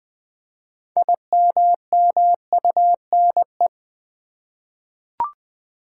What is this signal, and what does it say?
Morse code, heard as a keyed tone, about 20 wpm, 700 Hz. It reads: IMMUNE
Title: 6 Letter Words - First 3 Letters Given - 20wpm